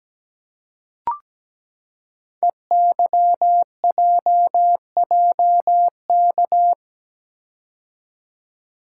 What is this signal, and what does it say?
Morse code, heard as a keyed tone, about 17 wpm, 700 Hz. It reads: EYJJK
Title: Code Groups - 5 Letters - 17wpm